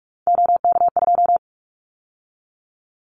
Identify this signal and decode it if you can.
Morse code, heard as a keyed tone, about 44 wpm, 700 Hz. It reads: OK2
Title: Callsign Prefixes for Top 100 DXCC - 45wpm